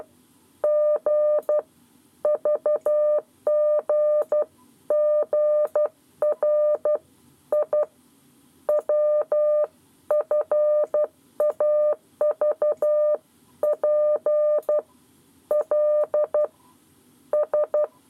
Morse code, heard as a keyed tone, about 12 wpm, 600 Hz. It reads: G VGGR I WFAVP L S
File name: 7 Lezione cw 1